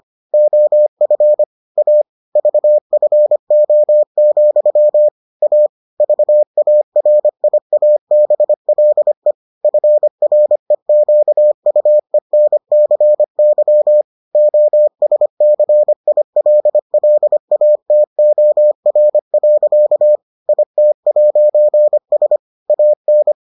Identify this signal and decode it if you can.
Morse code, heard as a keyed tone, about 25 wpm, 600 Hz. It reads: OF A VFO, A VARIABLE FREQUENCY OSCILLATOR. IT'S AN